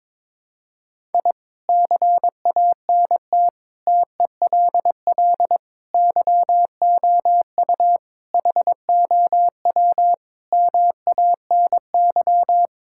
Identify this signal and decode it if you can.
Morse code, heard as a keyed tone, about 22 wpm, 700 Hz. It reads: I CANT TELL YOU HOW MANY
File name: practice / 22wpm / Sets of 8 Words - 22wpm